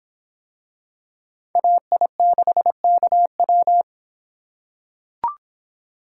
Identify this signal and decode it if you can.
Morse code, heard as a keyed tone, about 26 wpm, 700 Hz. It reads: AI6KW